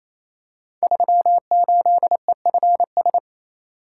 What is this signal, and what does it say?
Morse code, heard as a keyed tone, about 27 wpm, 700 Hz. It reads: 38EFS
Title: Code Groups - 5 Letter:Numbers - 28wpm